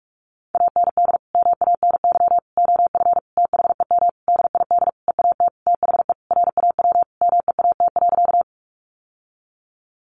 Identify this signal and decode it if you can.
Morse code, heard as a keyed tone, about 44 wpm, 700 Hz. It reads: AND MANY OF THEM DID EAT THE RAW MEAT.
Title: Sentences from Top 1500 Words 45wpm